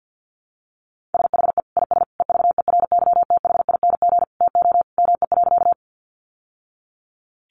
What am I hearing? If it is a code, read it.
Morse code, heard as a keyed tone, about 50 wpm, 700 Hz. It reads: SHE IS EVERYTHING TO ME.
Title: Sentences from Top 700 Words 50wpm